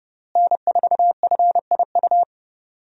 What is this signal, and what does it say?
Morse code, heard as a keyed tone, about 30 wpm, 700 Hz. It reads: N4FIU
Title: USA Call Signs - 5 Characters - Repeated 2x - 30wpm